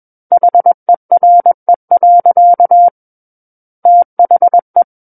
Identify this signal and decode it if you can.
Morse code, heard as a keyed tone, about 21 wpm, 700 Hz. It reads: HERE. THE
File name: Book One / 12-Chapter 12 - 21 WPM